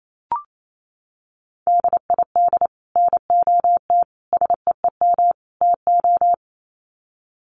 Morse code, heard as a keyed tone, about 28 wpm, 700 Hz. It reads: DID NOT SEEM TO
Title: Sets of 4 Words 28wpm